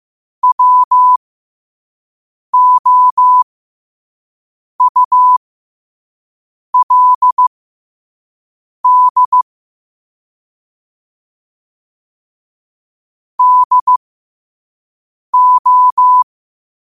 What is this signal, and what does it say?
Morse code, heard as a keyed tone, about 15 wpm, spaced as for 5 wpm, 1000 Hz. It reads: WOULD DO
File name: qotdfiles / 5wpm